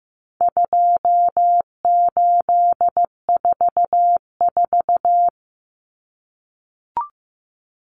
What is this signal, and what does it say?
Morse code, heard as a keyed tone, about 15 wpm, 700 Hz. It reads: 2844
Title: Numbers - 4-Digit 15wpm